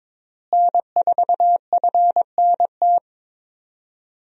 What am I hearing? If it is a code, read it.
Morse code, heard as a keyed tone, about 22 wpm, 700 Hz. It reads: N4FNT
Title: USA Call Signs - 5 Characters - Repeated 2x - 22wpm